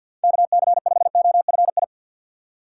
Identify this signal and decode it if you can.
Morse code, heard as a keyed tone, about 45 wpm, 700 Hz. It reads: KC5OFI
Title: USA Call Signs - 6 Characters - 4x Character Spacing - 50wpm